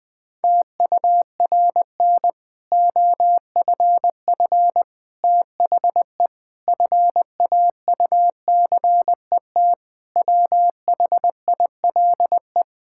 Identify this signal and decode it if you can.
Morse code, heard as a keyed tone, about 20 wpm, 700 Hz. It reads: TURN OFF THE FAUCET WHILE
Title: Difficult Sentences with Hints - 20wpm